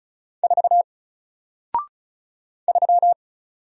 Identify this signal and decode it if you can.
Morse code, heard as a keyed tone, about 36 wpm, 700 Hz. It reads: 4 3
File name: Single-Letter-Number-35wpm